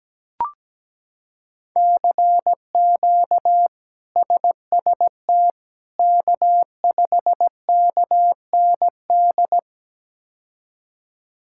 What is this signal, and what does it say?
Morse code, heard as a keyed tone, about 17 wpm, 700 Hz. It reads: CQ SST K5KND
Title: K1USN SST QSO Practice - 17wpm